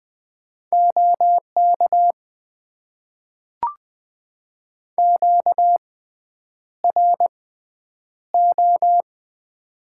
Morse code, heard as a keyed tone, about 20 wpm, spaced as for 9 wpm, 700 Hz. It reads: OK QRO
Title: Top 90 QSO Elements - 6x Character Spacing - 20wpm